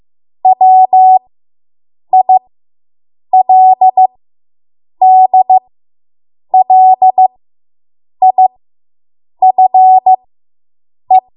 Morse code, heard as a keyed tone, about 15 wpm, spaced as for 7 wpm, 750 Hz. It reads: WILDLIFE